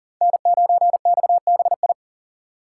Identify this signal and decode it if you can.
Morse code, heard as a keyed tone, about 50 wpm, 700 Hz. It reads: N9XBI